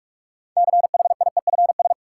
Morse code, heard as a keyed tone, about 43 wpm, 700 Hz. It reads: CHIEFS
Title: Sets of 5 Words Encore - New Word Drill - 45wpm